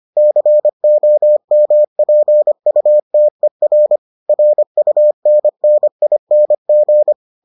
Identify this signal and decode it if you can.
Morse code, heard as a keyed tone, about 25 wpm, 600 Hz. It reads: COMPUTER RUNNING